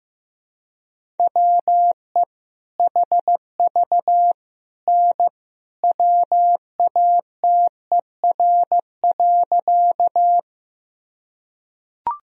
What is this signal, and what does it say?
Morse code, heard as a keyed tone, about 15 wpm, 700 Hz. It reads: WE HV N WATER.
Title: Sentences from Top 100 Words and Abbreviations 15wpm